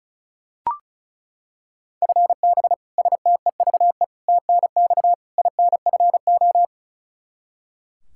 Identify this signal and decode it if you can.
Morse code, heard as a keyed tone, about 36 wpm, 700 Hz. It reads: FB STEVE TNX INFO
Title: Practice QSO - Exchange 3 - More Info - 35wpm